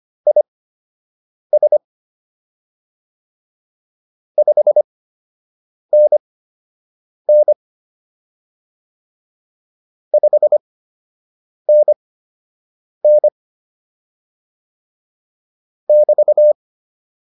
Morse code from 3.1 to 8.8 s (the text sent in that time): 5NN